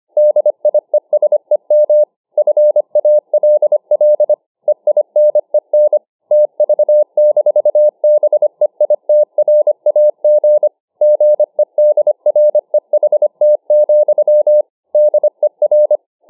Morse code, heard as a keyed tone, about 25 wpm, 600 Hz. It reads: DIESEM FALL EINEN TV-BEITRAG GEDREHT, DER